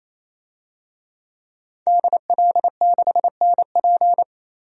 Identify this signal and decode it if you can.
Morse code, heard as a keyed tone, about 28 wpm, 700 Hz. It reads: DL6NP
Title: DXCC-100 Call Signs - 5 Characters - 28wpm